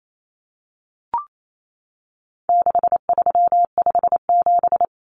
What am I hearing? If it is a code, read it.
Morse code, heard as a keyed tone, about 28 wpm, 700 Hz. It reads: 6357
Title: Common Cut Numbers - 4-Digit 28wpm